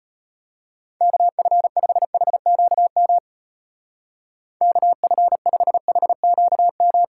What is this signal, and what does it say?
Morse code, heard as a keyed tone, about 31 wpm, 700 Hz. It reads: KF5HQM KF5HQM
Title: Speed Racing - USA 2x3 Call Signs 25wpm